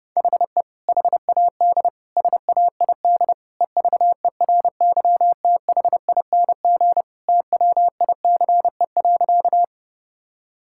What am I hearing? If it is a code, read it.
Morse code, heard as a keyed tone, about 30 wpm, 700 Hz. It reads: HE HAD SAID EVERYTHING TWICE.
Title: Sentences from Top 1600 Words 30wpm